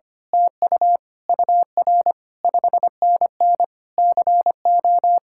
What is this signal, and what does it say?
Morse code, heard as a keyed tone, about 24 wpm, 700 Hz. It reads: TU UR 5NN CO
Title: Speed-Racing Common QSO Phrases Encore - 25wpm